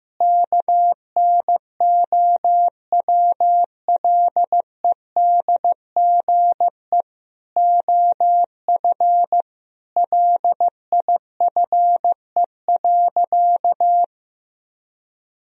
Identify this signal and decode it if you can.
Morse code, heard as a keyed tone, about 16 wpm, 700 Hz. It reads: KNOWLEDGE OF LIFE.
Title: Sentences from Top 1000 Words - Review Part 3 - 15wpm